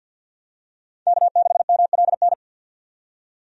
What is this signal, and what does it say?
Morse code, heard as a keyed tone, about 51 wpm, 700 Hz. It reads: K6MLN
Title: USA Call Signs - 5 Characters - 50wpm